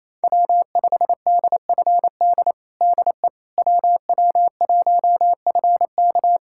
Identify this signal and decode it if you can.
Morse code, heard as a keyed tone, about 28 wpm, 700 Hz. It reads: W5DFD DE WW1FK
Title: USA Callsign DE Callsign - 5 Characters - 28wpm